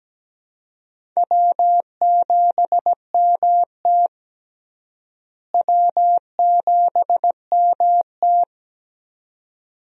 Morse code, heard as a keyed tone, about 17 wpm, 700 Hz. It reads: W7MT W7MT
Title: USA Call Signs - 4 Characters - Repeated 2x - 17wpm